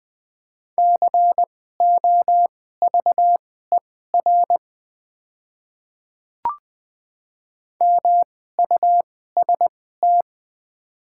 Morse code, heard as a keyed tone, about 20 wpm, spaced as for 15 wpm, 700 Hz. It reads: COVER MUST
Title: 2x Character Spacing - Top 300 Words - 20wpm